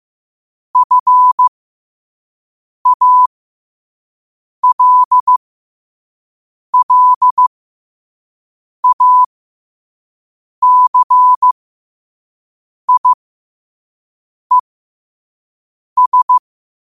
Morse code, heard as a keyed tone, about 15 wpm, spaced as for 5 wpm, 1000 Hz. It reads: FALLACIES
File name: qotdfiles / 5wpm